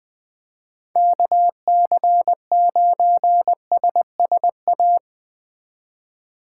Morse code, heard as a keyed tone, about 19 wpm, 700 Hz. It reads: KC9SSA
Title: Speed Racing - USA Call Signs 17wpm